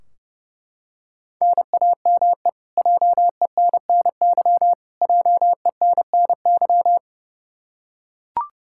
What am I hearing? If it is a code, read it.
Morse code, heard as a keyed tone, about 32 wpm, 700 Hz. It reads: NAME JENNY JENNY
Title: Practice QSO - Exchange 1 - Beginning - 30wpm